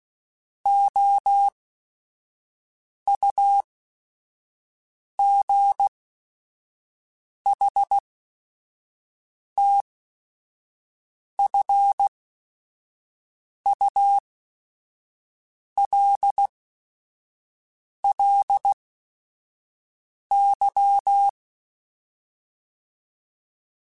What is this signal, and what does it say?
Morse code, heard as a keyed tone, about 16 wpm, spaced as for 5 wpm, 800 Hz. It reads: OUGHTFULLY